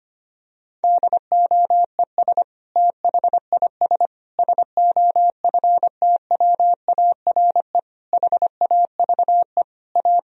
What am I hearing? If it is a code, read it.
Morse code, heard as a keyed tone, about 25 wpm, 700 Hz. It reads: DOES THIS SOFTWARE HAVE A
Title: Sentences from Top 1000 Words - Review Part 3 - 25wpm